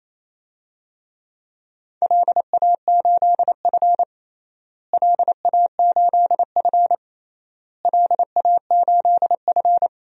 Morse code, heard as a keyed tone, about 27 wpm, 700 Hz. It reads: LA8F LA8F LA8F
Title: DXCC-100 Call Signs - 4 Characters - Repeated 3x - 28wpm